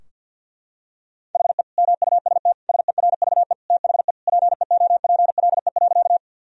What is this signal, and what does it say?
Morse code, heard as a keyed tone, about 50 wpm, 700 Hz. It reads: HE MUST SERVE THE PEOPLE.
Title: Sentences from Top 1000 Words - Review Part 2 - 50wpm